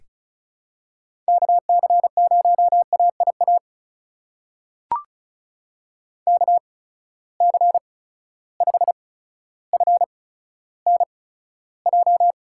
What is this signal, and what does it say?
Morse code, heard as a keyed tone, about 36 wpm, spaced as for 15 wpm, 700 Hz. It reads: KC0AIA KC5FNJ
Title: USA Call Signs - 6 Characters - 8x Character Spacing - 35wpm